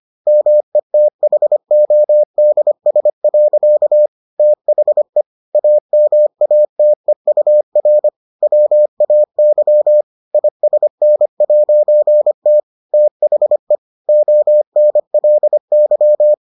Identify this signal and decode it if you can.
Morse code, heard as a keyed tone, about 25 wpm, 600 Hz. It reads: METHODS. THE AMATEUR WAY ISN'T THE ONLY